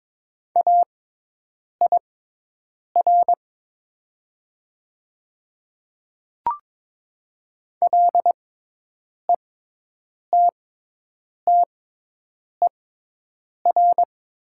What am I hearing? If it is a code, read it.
Morse code, heard as a keyed tone, about 22 wpm, spaced as for 8 wpm, 700 Hz. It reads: AIR LETTER
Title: 6x Character Spacing - Top 300 Words - 22wpm